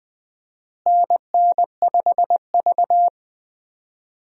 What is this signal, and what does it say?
Morse code, heard as a keyed tone, about 20 wpm, 690 Hz. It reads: NN5V
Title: USA Call Signs - 4 Characters - Repeated 3x - 20wpm